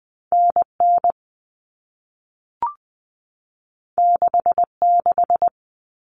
Morse code, heard as a keyed tone, about 20 wpm, 700 Hz. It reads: NN 66
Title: All Cut Numbers - 2-Digit 20wpm